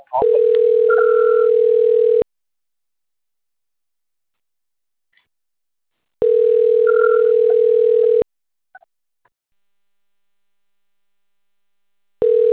Ringback tone (North America), 440 + 480 Hz.